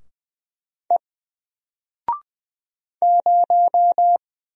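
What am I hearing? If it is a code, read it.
Morse code, heard as a keyed tone, about 20 wpm, 700 Hz. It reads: E 0
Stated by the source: Single letter and number Morse drill at 20 wpm, from N4LCD.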